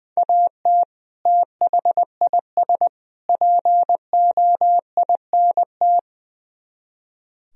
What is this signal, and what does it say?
Morse code, heard as a keyed tone, about 21 wpm, 700 Hz. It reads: AT THIS POINT
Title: Speed-Racing - Sets of 3 Words 20wpm